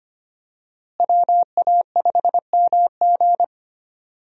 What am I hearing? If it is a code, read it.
Morse code, heard as a keyed tone, about 25 wpm, 700 Hz. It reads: WA5MG
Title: USA Call Signs - 5 Characters - Repeated 2x - 25wpm